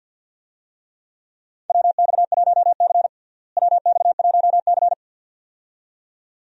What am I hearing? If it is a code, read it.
Morse code, heard as a keyed tone, about 51 wpm, 700 Hz. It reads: WX1C WX1C
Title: USA Call Signs - 4 Characters - Repeated 2x - 50wpm